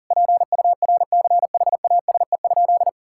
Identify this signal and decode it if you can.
Morse code, heard as a keyed tone, about 40 wpm, 700 Hz. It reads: PURCHASE?
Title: Sentences from Top 1000 Words - Review Part 6 - 40wpm